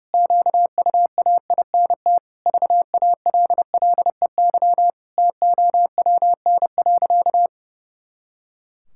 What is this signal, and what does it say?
Morse code, heard as a keyed tone, about 30 wpm, 700 Hz. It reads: QUAINT VALLEY TOWN.
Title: Pangram Challenge v2 - 30wpm